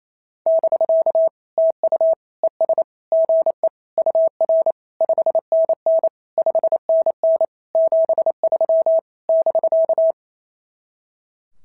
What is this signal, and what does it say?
Morse code, heard as a keyed tone, about 26 wpm, 650 Hz. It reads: <BK> TU ES GE UR 5NN 5NN 73 <BK>
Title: SOTA QSO Practice 28wpm